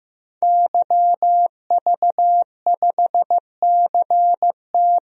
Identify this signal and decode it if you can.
Morse code, heard as a keyed tone, about 15 wpm, 700 Hz. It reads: YV5CT
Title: DXCC-100 Call Signs - 5 Characters - 15wpm